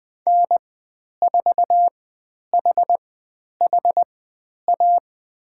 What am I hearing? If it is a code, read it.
Morse code, heard as a keyed tone, about 20 wpm, spaced as for 10 wpm, 700 Hz. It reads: N4HHA
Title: 109 Lesson 18g - Letter m - Call Signs 20wpm